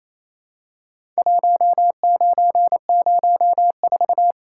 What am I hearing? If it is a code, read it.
Morse code, heard as a keyed tone, about 28 wpm, 700 Hz. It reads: 1904
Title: Worked All Europe DX Contest - QTC Exchange - 28wpm